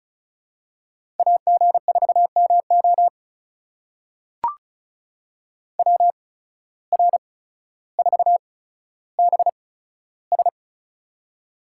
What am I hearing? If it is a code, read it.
Morse code, heard as a keyed tone, about 36 wpm, spaced as for 15 wpm, 700 Hz. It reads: AG4MO WR4BS
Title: USA Call Signs - 5 Characters - 8x Character Spacing - 35wpm